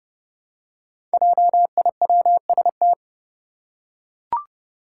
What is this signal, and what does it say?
Morse code, heard as a keyed tone, about 29 wpm, 700 Hz. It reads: JIWST
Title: Five Challenging Letter-Number Combinations 30wpm